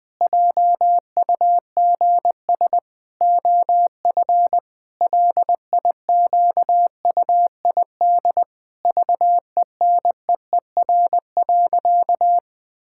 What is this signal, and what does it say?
Morse code, heard as a keyed tone, about 19 wpm, 700 Hz. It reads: JUGS OF LIQUID VENEER.
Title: Pangram Challenge v2 - 20wpm